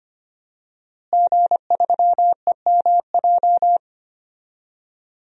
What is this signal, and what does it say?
Morse code, heard as a keyed tone, about 25 wpm, 700 Hz. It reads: G3EMJ
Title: DXCC-100 Call Signs - 5 Characters - Repeated 3x - 25wpm